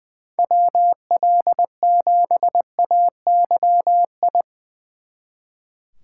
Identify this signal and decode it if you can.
Morse code, heard as a keyed tone, about 22 wpm, 700 Hz. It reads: WL7AYI
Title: Speed Racing - USA Call Signs 20wpm